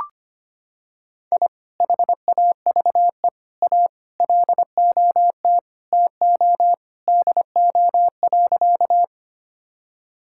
Morse code, heard as a keyed tone, about 25 wpm, 700 Hz. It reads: I HAVE A LOT TO DO.
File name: practice / 25wpm / Sentences with 2-Word Phrases - 25wpm